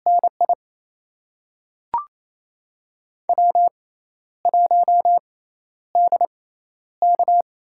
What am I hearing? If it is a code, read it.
Morse code, heard as a keyed tone, about 28 wpm, spaced as for 12 wpm, 700 Hz. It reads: NI W1DK
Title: USA Call Signs - 6x Character Spacing - 28wpm